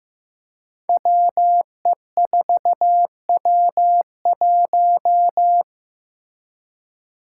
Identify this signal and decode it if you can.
Morse code, heard as a keyed tone, about 15 wpm, 700 Hz. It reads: WE4W1
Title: Five Challenging Letter-Number Combinations 15wpm